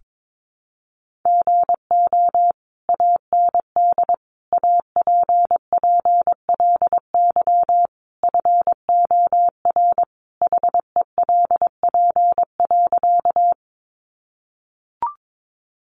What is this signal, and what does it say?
Morse code, heard as a keyed tone, about 22 wpm, 700 Hz. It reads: GO AND APPLY FOR HELP.
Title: Sentences from Top 1200 Words 22wpm